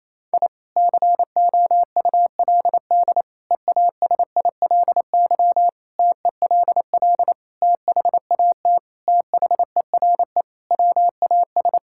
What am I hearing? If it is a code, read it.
Morse code, heard as a keyed tone, about 27 wpm, 700 Hz. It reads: I COULD EASILY TELL THAT THERE WAS
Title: Sentences from Top 1000 Words - Review Part 6 - 28wpm